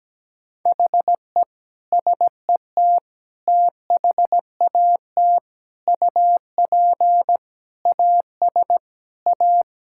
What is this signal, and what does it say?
Morse code, heard as a keyed tone, about 17 wpm, 700 Hz. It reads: HE SET THAT UP AS A
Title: Sentences from Top 1000 Words - Review Part 6 - 17wpm